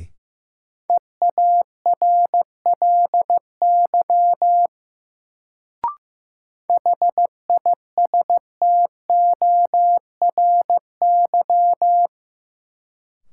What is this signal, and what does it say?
Morse code, heard as a keyed tone, about 15 wpm, 700 Hz. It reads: EARLY HISTORY
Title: Sets of 3 Words - New Word Drill - 15wpm